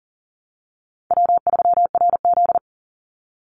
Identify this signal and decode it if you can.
Morse code, heard as a keyed tone, about 40 wpm, 700 Hz. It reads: W3RZ
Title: Easy Call Signs Repeated 2x - 40wpm